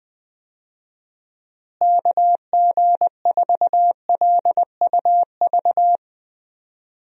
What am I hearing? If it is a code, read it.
Morse code, heard as a keyed tone, about 20 wpm, 700 Hz. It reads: KG4LUV
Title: USA Call Signs - 6 Characters - Repeated 3x - 20wpm